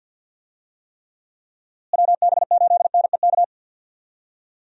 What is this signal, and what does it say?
Morse code, heard as a keyed tone, about 51 wpm, 700 Hz. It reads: WB8NEK
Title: USA Call Signs - 6 Characters - Repeated 2x - 50wpm